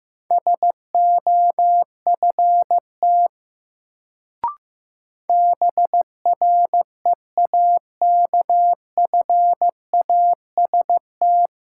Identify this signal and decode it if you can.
Morse code, heard as a keyed tone, about 15 wpm, 700 Hz. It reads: SOFT BREAKFAST
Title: Sentences from Top 1500 Words - New Word Drill - 15wpm